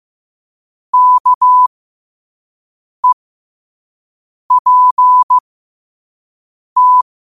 Morse code, heard as a keyed tone, about 15 wpm, spaced as for 5 wpm, 1000 Hz. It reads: KEPT
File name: qotdfiles / 5wpm